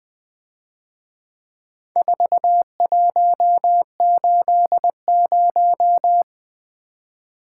Morse code, heard as a keyed tone, about 21 wpm, 700 Hz. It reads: 4180
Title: Speed-Racing - Four Letter-Number Combinations - 20wpm